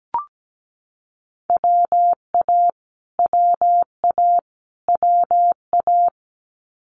Morse code, heard as a keyed tone, about 17 wpm, 700 Hz. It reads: WA WA WA
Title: Multi-letter Phonemes Repeated 3x 17wpm